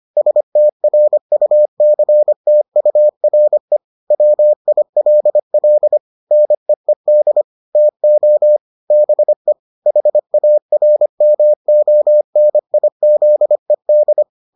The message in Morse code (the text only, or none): STRUCTURE WILL NEED TO BE HARMONIZED